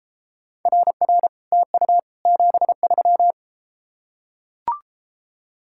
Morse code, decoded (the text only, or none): RR TU 73